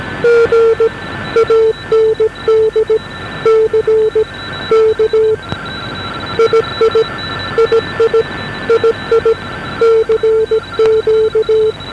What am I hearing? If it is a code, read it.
Morse code, heard as a keyed tone, about 17 wpm, 460 Hz. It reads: G AND C K II II II CQ
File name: OXZ41 8598 1141